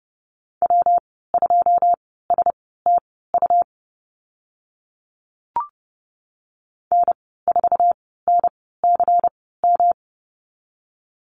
Morse code, heard as a keyed tone, about 30 wpm, spaced as for 17 wpm, 700 Hz. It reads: W2STU N4NCM
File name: Lesson 19g - Letter w - Call Signs 30wpm